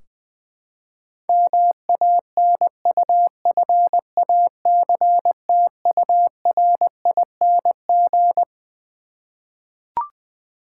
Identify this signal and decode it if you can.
Morse code, heard as a keyed tone, about 20 wpm, 700 Hz. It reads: MANUFACTURING
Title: K1USN SST QSO Practice 20wpm